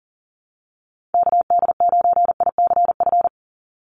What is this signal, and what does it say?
Morse code, heard as a keyed tone, about 40 wpm, 700 Hz. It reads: KD9ICF
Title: USA Call Signs 40wpm